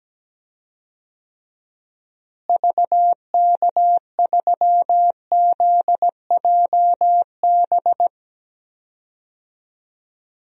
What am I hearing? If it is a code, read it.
Morse code, heard as a keyed tone, about 17 wpm, 700 Hz. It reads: VK3ZJB
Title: DXCC-100 Call Signs - 6 Characters - Repeated 2x - 17wpm